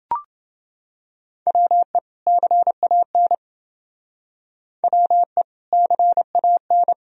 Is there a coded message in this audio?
Yes: Morse code from 1.5 to 6.9 s, heard as a keyed tone, about 28 wpm, 700 Hz. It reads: WE CAN WE CAN